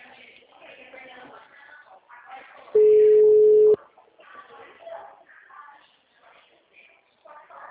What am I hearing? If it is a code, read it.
Ringback tone (Europe, 425 Hz), with speech.
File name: out- -2003